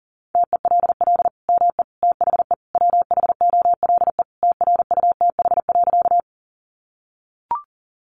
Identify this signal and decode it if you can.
Morse code, heard as a keyed tone, about 40 wpm, 700 Hz. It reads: TELL ME THE WHOLE TRUTH.
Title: Sentences from Top 800 Words 40wpm